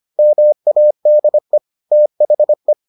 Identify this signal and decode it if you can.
Morse code, heard as a keyed tone, about 25 wpm, 600 Hz. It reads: MADE THE